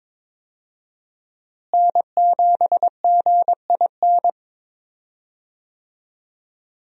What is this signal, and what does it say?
Morse code, heard as a keyed tone, about 22 wpm, 700 Hz. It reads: N7GIN
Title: USA Call Signs - 5 Characters - Repeated 3x - 22wpm